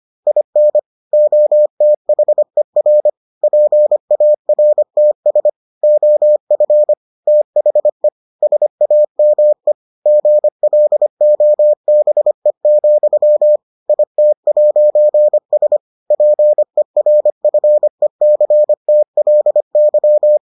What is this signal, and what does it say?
Morse code, heard as a keyed tone, about 25 wpm, 600 Hz. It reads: IN OTHER PARTS OF THE SAME GLOBE, IT'S PERFECTLY